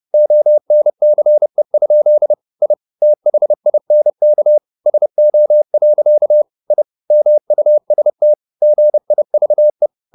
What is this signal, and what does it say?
Morse code, heard as a keyed tone, about 30 wpm, 600 Hz. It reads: ONCE? I THINK SO. I MUST GIVE
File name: the old man and the sea / the old man and the sea 1 30wpm